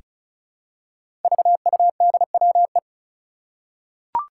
Morse code, heard as a keyed tone, about 36 wpm, 700 Hz. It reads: VUDWE
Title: Five Challenging Letter-Number Combinations 35wpm